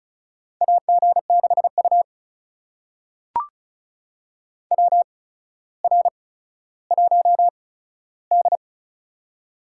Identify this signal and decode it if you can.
Morse code, heard as a keyed tone, about 36 wpm, spaced as for 15 wpm, 700 Hz. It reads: AG6U WR1D
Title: USA Call Signs - 4 Characters - 8x Character Spacing - 35wpm